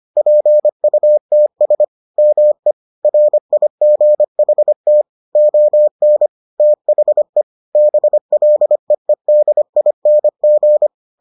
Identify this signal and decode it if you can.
Morse code, heard as a keyed tone, about 25 wpm, 600 Hz. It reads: PUTS ME RIGHT ON THE BLEEDING